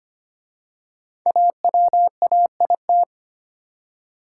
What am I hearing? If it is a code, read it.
Morse code, heard as a keyed tone, about 25 wpm, 700 Hz. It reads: AWAIT
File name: 5 Letter Words - First 3 Letters Given - 25wpm